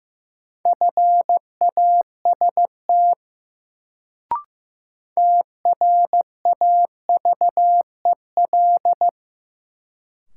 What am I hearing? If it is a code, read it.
Morse code, heard as a keyed tone, about 15 wpm, 700 Hz. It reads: FAST TRAVEL
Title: Sentences from Top 400 Words - New Word Drill - 15wpm